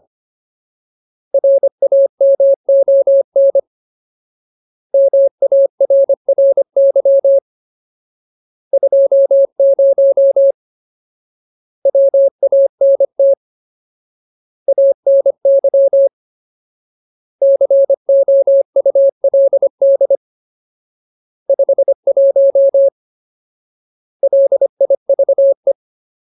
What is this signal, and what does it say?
Morse code, heard as a keyed tone, about 25 wpm, 550 Hz. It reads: RAMON MARRY 20 WANT ANY COULD 51 LIVE
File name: Ultimate Word List / 25WPM Ultimate